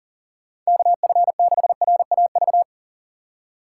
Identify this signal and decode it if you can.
Morse code, heard as a keyed tone, about 40 wpm, 700 Hz. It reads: KF6RAV
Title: USA Call Signs - 3x Character Spacing - 40wpm